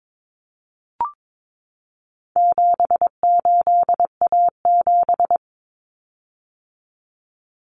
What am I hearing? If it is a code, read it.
Morse code, heard as a keyed tone, about 22 wpm, 700 Hz. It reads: 78A7
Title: Common Cut Numbers - 4-Digit 22wpm